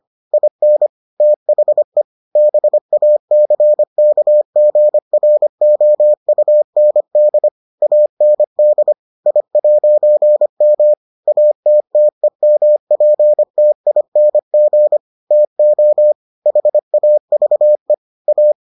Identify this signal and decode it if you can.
Morse code, heard as a keyed tone, about 25 wpm, 600 Hz. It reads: IN THE BACKGROUND AND I'M ATTEMPTING TO HAVE A